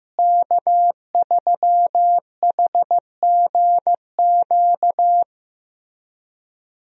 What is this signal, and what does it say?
Morse code, heard as a keyed tone, about 15 wpm, 700 Hz. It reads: K3HGQ
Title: USA Call Signs - 5 Characters - Repeated 3x - 15wpm